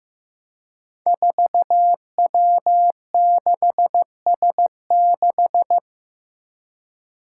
Morse code, heard as a keyed tone, about 15 wpm, 700 Hz. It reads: 4W6S6
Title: Five Challenging Letter-Number Combinations 15wpm